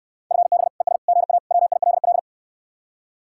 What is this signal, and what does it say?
Morse code, heard as a keyed tone, about 66 wpm, 700 Hz. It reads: FB ES GA WILL